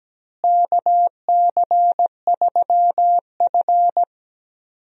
Morse code, heard as a keyed tone, about 17 wpm, 700 Hz. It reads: KC3F
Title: USA Call Signs - 4 Characters - Repeated 3x - 17wpm